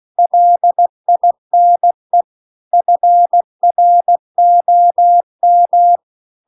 Morse code, heard as a keyed tone, about 16 wpm, 700 Hz. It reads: LINE FROM